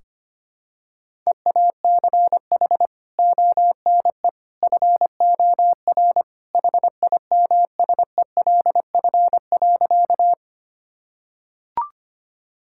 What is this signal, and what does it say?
Morse code, heard as a keyed tone, about 25 wpm, 700 Hz. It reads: EACH ONE FOR HIMSELF.
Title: Sentences from Top 1600 Words 25wpm